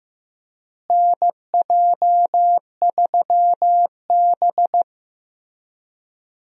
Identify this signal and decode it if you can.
Morse code, heard as a keyed tone, about 16 wpm, 700 Hz. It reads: NJ3B
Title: USA Call Signs - 4 Characters - Repeated 3x - 15wpm